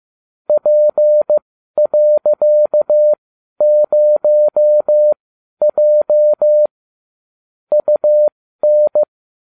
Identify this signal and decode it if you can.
Morse code, heard as a keyed tone, about 15 wpm, spaced as for 12 wpm, 600 Hz. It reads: P.0J UN